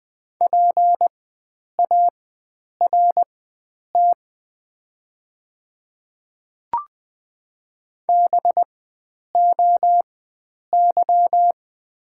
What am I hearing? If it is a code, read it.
Morse code, heard as a keyed tone, about 20 wpm, spaced as for 9 wpm, 700 Hz. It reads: PART BOY
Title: 4x Character Spacing - Top 500 Words - 20wpm